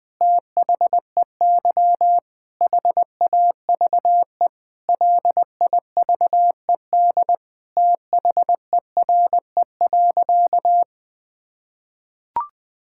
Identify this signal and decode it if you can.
Morse code, heard as a keyed tone, about 20 wpm, 700 Hz. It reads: THEY HAVE LIVED THERE.
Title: Sentences with 7-Word Phrases - Spoken Beginning - 20wpm